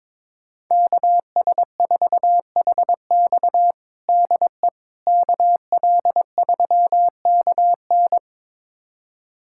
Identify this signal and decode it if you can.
Morse code, heard as a keyed tone, about 21 wpm, 700 Hz. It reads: KS4HX DE KL3KN
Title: USA Callsign DE Callsign - 5 Characters - 22wpm